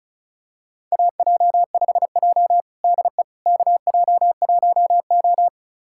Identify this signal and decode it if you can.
Morse code, heard as a keyed tone, about 36 wpm, 700 Hz. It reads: AJ5J DE KJ1O
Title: USA Callsign DE Callsign - 4 Characters - 35wpm